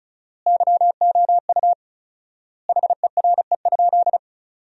Morse code, heard as a keyed tone, about 36 wpm, 700 Hz. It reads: YOU HERE?
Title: 4x Word-Spacing - Sentences from Top 200 Words - 35wpm